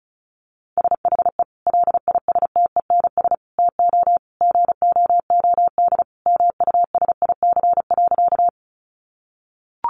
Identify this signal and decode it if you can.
Morse code, heard as a keyed tone, about 34 wpm, 700 Hz. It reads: SHE LISTENS TO GOOD MUSIC.
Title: Easy Phrases 35wpm